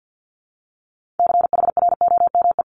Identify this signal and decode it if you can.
Morse code, heard as a keyed tone, about 50 wpm, 700 Hz. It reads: CHROME
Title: Multi-letter Phonemes 50wpm